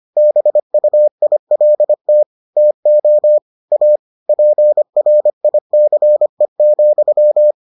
Morse code, heard as a keyed tone, about 25 wpm, 600 Hz. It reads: BUILT TO A PRICE,